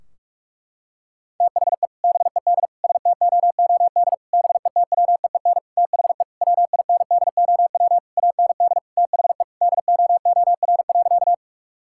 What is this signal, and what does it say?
Morse code, heard as a keyed tone, about 46 wpm, 700 Hz. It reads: THE BED STOOD BETWEEN THE WINDOW AND THE DOOR.
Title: Sentences from Top 1600 Words 45wpm